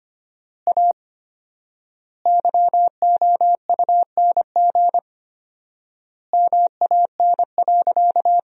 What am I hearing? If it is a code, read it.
Morse code, heard as a keyed tone, about 25 wpm, 700 Hz. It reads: A YOUNG MAN.
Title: 4x Word-Spacing - Sentences from Top 500 Words - 25wpm